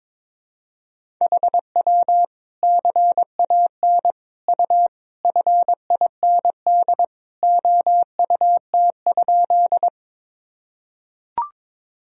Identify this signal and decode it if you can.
Morse code, heard as a keyed tone, about 22 wpm, 700 Hz. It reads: HW CAN U FIND OUT?
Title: Sentences from Top 200 Words and Abbreviations 22wpm